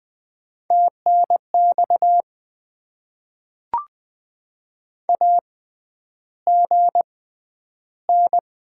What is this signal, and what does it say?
Morse code, heard as a keyed tone, about 20 wpm, spaced as for 10 wpm, 700 Hz. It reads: TNX AGN
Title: Top 20 QSO Elements - 6x Character Spacing - 20wpm